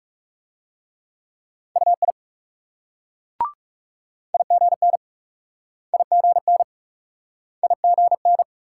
Morse code, heard as a keyed tone, about 38 wpm, 700 Hz. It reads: UI IGN IGN IGN